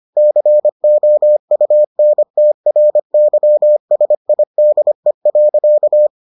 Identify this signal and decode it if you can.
Morse code, heard as a keyed tone, about 25 wpm, 600 Hz. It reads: COUNTRYSIDE.